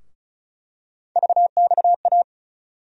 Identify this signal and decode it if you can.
Morse code, heard as a keyed tone, about 34 wpm, 700 Hz. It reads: VXA
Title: USA Call Sign Postfixes - 3 Letters 35wpm